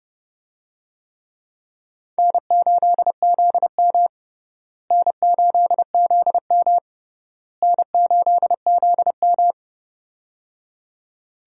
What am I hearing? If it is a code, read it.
Morse code, heard as a keyed tone, about 30 wpm, 700 Hz. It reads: N8ZM N8ZM N8ZM
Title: USA Call Signs - 4 Characters - Repeated 3x - 30wpm